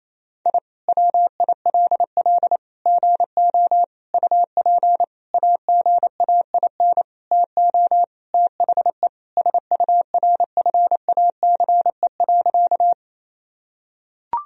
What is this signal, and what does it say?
Morse code, heard as a keyed tone, about 28 wpm, 700 Hz. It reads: I WILL GO UP AGAIN TO THE SURFACE.
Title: Sentences from Top 1000 Words - Review Part 3 - 28wpm